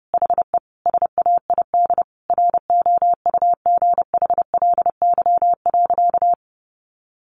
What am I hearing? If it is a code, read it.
Morse code, heard as a keyed tone, about 30 wpm, 700 Hz. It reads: HE SAID ROUGHLY.
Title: Sentences from Top 1500 Words 30wpm